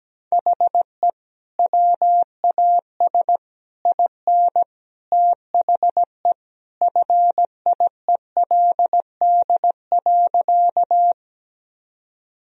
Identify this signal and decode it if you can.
Morse code, heard as a keyed tone, about 17 wpm, 700 Hz. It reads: HE WAS IN THE FIELD.
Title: Sentences from Top 1000 Words - Review Part 2 - 17wpm